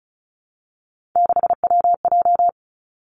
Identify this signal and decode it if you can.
Morse code, heard as a keyed tone, about 36 wpm, 700 Hz. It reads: 6WJ
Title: Three Challenging Letter-Number Combinations 35wpm